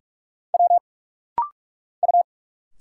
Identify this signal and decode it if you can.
Morse code, heard as a keyed tone, about 46 wpm, 700 Hz. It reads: W U
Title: Single Letters - Rapid-Fire 45wpm